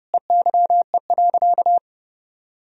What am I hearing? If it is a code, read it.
Morse code, heard as a keyed tone, about 30 wpm, 700 Hz. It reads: EYE.
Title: Sentences from Top 1000 Words - Review Part 2 - 30wpm